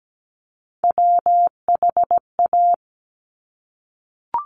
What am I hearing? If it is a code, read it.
Morse code, heard as a keyed tone, about 17 wpm, 700 Hz. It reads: WHA
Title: Top Three Letter Combinations in Words 17wpm